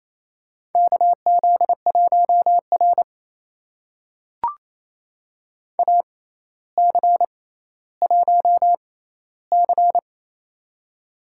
Morse code, heard as a keyed tone, about 28 wpm, spaced as for 14 wpm, 700 Hz. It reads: KZ1R AC1C
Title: USA Call Signs - 4 Characters - 6x Character Spacing - 28wpm